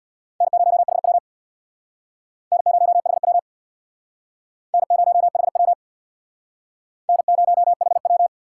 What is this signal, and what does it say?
Morse code, heard as a keyed tone, about 62 wpm, 700 Hz. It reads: N0HW N0HW N0HW N0HW
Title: Speed Racing - USA 1x2 Call Signs 50wpm